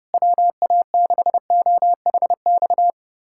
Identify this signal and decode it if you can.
Morse code, heard as a keyed tone, about 30 wpm, 700 Hz. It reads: WA6OHX
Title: USA Callsign DE Callsign - 6 Characters - 30wpm